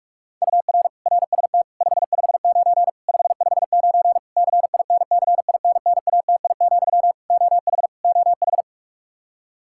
Morse code, heard as a keyed tone, about 46 wpm, 700 Hz. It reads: UR RST 559 559 CINCINNATI, OH OH
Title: Practice QSO - Exchange 1 - Beginning - 45wpm